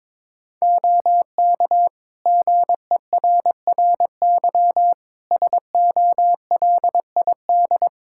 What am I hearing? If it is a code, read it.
Morse code, heard as a keyed tone, about 22 wpm, 700 Hz. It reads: OK GERRY SOLID
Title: Practice QSO - Exchange 3 - More Info - 22wpm